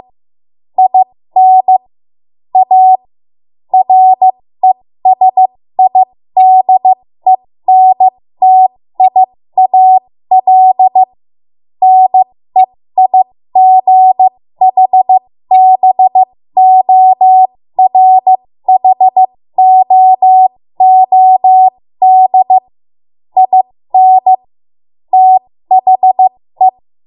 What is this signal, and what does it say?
Morse code, heard as a keyed tone, about 15 wpm, spaced as for 13 wpm, 750 Hz. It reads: IN A RESIDENTIAL NEIGHBORHOOD IN THE
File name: Morse / 13WPM